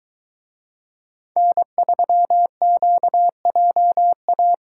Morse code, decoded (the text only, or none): N3QJA